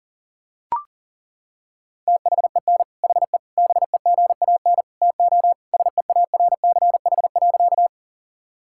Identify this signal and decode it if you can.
Morse code, heard as a keyed tone, about 40 wpm, 700 Hz. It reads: THEN HE BEGAN TO SEARCH.
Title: Sentences from Top 1000 Words - Review Part 5 - 40wpm